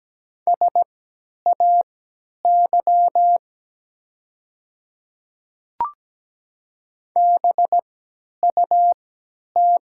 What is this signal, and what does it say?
Morse code, heard as a keyed tone, about 17 wpm, spaced as for 10 wpm, 700 Hz. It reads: SAY BUT
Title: 3x Character Spacing - Top 200 Words - 17wpm